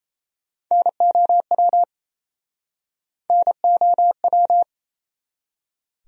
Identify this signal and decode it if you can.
Morse code, heard as a keyed tone, about 37 wpm, 700 Hz. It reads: NOW NOW